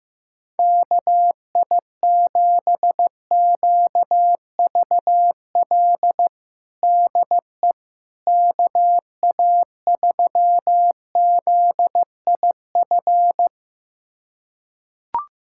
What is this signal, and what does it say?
Morse code, heard as a keyed tone, about 16 wpm, 700 Hz. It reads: KI7QVL DE KA3ZIF
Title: USA Callsign DE Callsign 15wpm